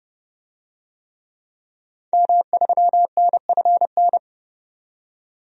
Morse code, heard as a keyed tone, about 30 wpm, 700 Hz. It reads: M3NFN
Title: DXCC-100 Call Signs - 5 Characters - 30wpm